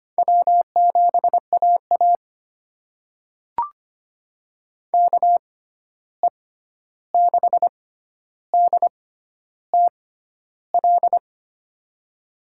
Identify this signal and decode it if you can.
Morse code, heard as a keyed tone, about 25 wpm, spaced as for 11 wpm, 700 Hz. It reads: W7AA KE6DTL
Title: USA Call Signs - 6x Character Spacing - 25wpm